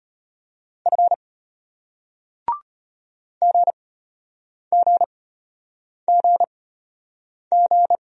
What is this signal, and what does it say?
Morse code, heard as a keyed tone, about 34 wpm, spaced as for 8 wpm, 700 Hz. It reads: F GGGG